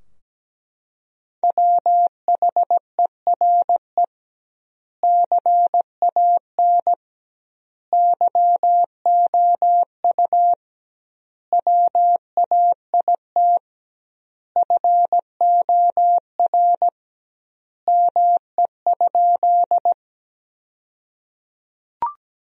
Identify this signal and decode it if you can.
Morse code, heard as a keyed tone, about 17 wpm, 700 Hz. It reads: WHERE CAN YOU WAIT FOR ME?
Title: 2x Word-Spacing - Sentences from Top 500 Words - 17wpm